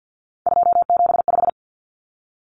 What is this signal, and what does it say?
Morse code, heard as a keyed tone, about 50 wpm, 700 Hz. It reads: 275
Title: Numbers - 3-Digit 50wpm